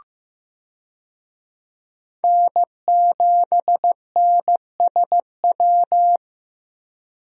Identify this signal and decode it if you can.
Morse code, heard as a keyed tone, about 15 wpm, 700 Hz. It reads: N7NSW
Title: USA Call Signs - 5 Characters - Repeated 2x - 15wpm